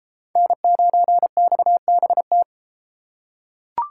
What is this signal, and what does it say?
Morse code, heard as a keyed tone, about 32 wpm, 700 Hz. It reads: N9XBT